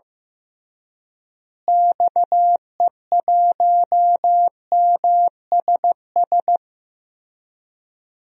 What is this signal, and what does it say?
Morse code, heard as a keyed tone, about 16 wpm, 700 Hz. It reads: XE1MSS
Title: DXCC-100 Call Signs - 6 Characters - Repeated 3x - 15wpm